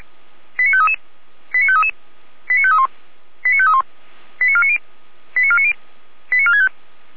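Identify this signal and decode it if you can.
Five-tone selective call (ZVEI), heard as quick runs of five tones, tones of about 80 ms. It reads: 89533 89533 89531 89531 89490 89490 89465
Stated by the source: ZVEI five-tone selective call received on SDR.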